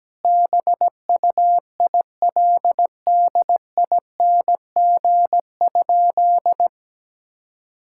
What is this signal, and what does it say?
Morse code, heard as a keyed tone, about 17 wpm, 700 Hz. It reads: BUILDING?
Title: Sentences from Top 1000 Words - Review Part 6 - 17wpm